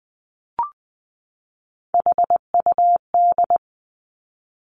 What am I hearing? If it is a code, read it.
Morse code, heard as a keyed tone, about 20 wpm, 700 Hz. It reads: HUD
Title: Three Challenging Letter-Number Combinations 20wpm